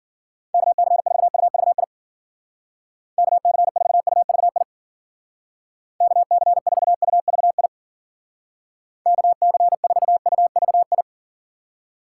Morse code, heard as a keyed tone, about 52 wpm, 700 Hz. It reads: KC4UVI KC4UVI KC4UVI KC4UVI